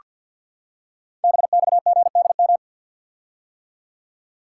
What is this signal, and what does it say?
Morse code, heard as a keyed tone, about 48 wpm, 700 Hz. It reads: BXGDM
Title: Code Groups - 5 Letters - 50wpm